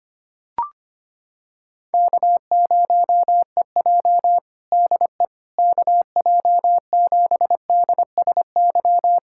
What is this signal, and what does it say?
Morse code, heard as a keyed tone, about 25 wpm, 700 Hz. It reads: K0EJ DE KJ7DSY
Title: Practice QSO - Exchange 4 - Ending - 25wpm